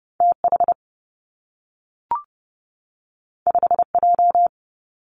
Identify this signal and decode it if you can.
Morse code, heard as a keyed tone, about 30 wpm, 700 Hz. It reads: T H 5 J
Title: Two Challenging Letter-Number Combinations 30wpm